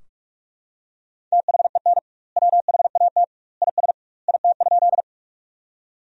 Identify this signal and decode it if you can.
Morse code, heard as a keyed tone, about 41 wpm, 700 Hz. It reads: THEN WHAT IS IT?
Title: 2x Word-Spacing - Sentences from Top 100 Words - 45wpm